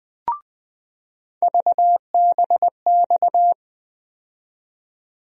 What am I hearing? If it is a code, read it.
Morse code, heard as a keyed tone, about 20 wpm, 700 Hz. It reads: VBX
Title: USA Call Sign Postfixes - 3 Letters 20wpm